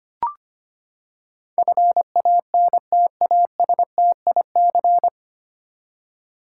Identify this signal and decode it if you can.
Morse code, heard as a keyed tone, about 25 wpm, 700 Hz. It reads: FANTASTIC
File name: Nine Letter Words - 25wpm